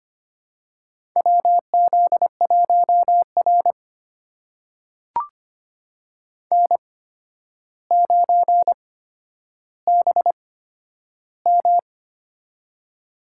Morse code, heard as a keyed tone, about 25 wpm, spaced as for 11 wpm, 700 Hz. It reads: WZ1R N9BM
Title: USA Call Signs - 4 Characters - 8x Character Spacing - 25wpm